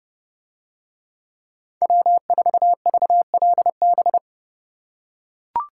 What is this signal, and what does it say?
Morse code, heard as a keyed tone, about 30 wpm, 700 Hz. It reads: W4VLB